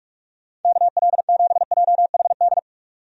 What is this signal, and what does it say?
Morse code, heard as a keyed tone, about 46 wpm, 700 Hz. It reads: KL7JHD
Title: USA Call Signs - 6 Characters - Repeated 3x - 45wpm